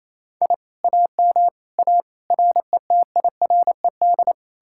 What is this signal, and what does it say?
Morse code, heard as a keyed tone, about 36 wpm, 700 Hz. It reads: I AM A RETIRED